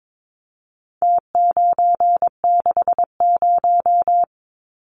Morse code, heard as a keyed tone, about 22 wpm, 700 Hz. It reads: T960
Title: Numbers - Contest Style - 4-Digit 22wpm